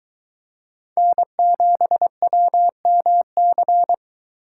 Morse code, heard as a keyed tone, about 23 wpm, 700 Hz. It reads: N7WMC